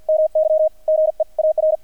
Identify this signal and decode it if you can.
Morse code, heard as a keyed tone, about 48 wpm, spaced as for 42 wpm, 630 Hz, with radio noise. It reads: MY GEAR
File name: archived / CRSnet